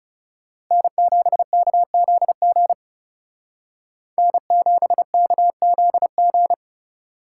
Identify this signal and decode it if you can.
Morse code, heard as a keyed tone, about 34 wpm, 700 Hz. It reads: N7KZG N7KZG
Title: Speed Racing - USA Call Signs 30wpm